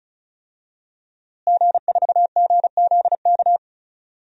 Morse code, heard as a keyed tone, about 34 wpm, 700 Hz. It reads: G4GZK
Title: DXCC-100 Call Signs - 5 Characters - Repeated 2x - 35wpm